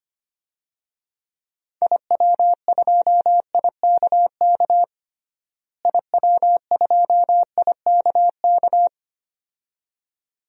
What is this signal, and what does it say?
Morse code, heard as a keyed tone, about 25 wpm, 700 Hz. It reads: IW2IKK IW2IKK
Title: DXCC-100 Call Signs - 6 Characters - Repeated 2x - 25wpm